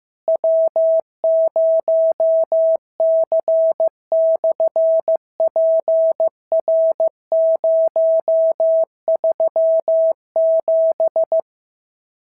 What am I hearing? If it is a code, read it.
Morse code, heard as a keyed tone, about 15 wpm, 650 Hz. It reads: W0C/PR037
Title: SOTA QSO Practice 15wpm